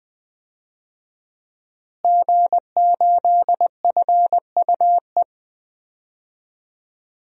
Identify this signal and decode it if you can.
Morse code, heard as a keyed tone, about 20 wpm, 700 Hz. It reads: G8FUE
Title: DXCC-100 Call Signs - 5 Characters - Repeated 3x - 20wpm